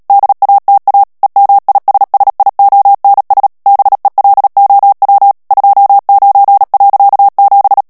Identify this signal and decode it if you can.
Morse code, heard as a keyed tone, about 37 wpm, 800 Hz. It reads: DATA EMISSIONS BELOW 29.7
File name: morse mp3 / ANGLAIS texte clair 3 vitesse 35 mots minute